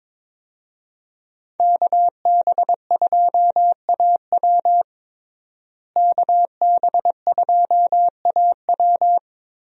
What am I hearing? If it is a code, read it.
Morse code, heard as a keyed tone, about 22 wpm, 700 Hz. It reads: KB2AW KB2AW
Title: USA Call Signs - 5 Characters - Repeated 3x - 22wpm